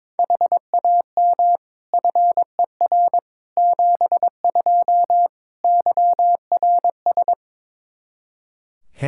Morse code, heard as a keyed tone, about 22 wpm, 700 Hz. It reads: HAM FER 72 YRS
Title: Practice QSO - Exchange 3 - More Info - 22wpm